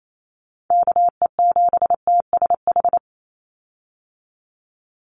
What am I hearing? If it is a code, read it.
Morse code, heard as a keyed tone, about 28 wpm, 700 Hz. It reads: KE7TSH
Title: USA Call Signs 28wpm